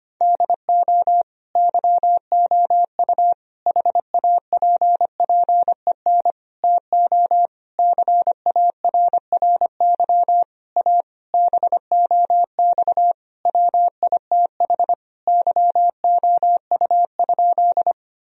Morse code, heard as a keyed tone, about 25 wpm, 700 Hz. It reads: DO YOU HAPPEN TO CARRY A BOX WITH YOU?
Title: Sentences from Top 1000 Words - Review Part 2 - 25wpm